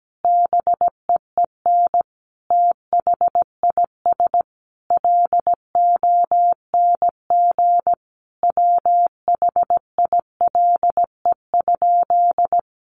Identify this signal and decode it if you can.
Morse code, heard as a keyed tone, about 17 wpm, 700 Hz. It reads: BEEN THIS LONG WHILE?
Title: Sentences from Top 300 Words 17wpm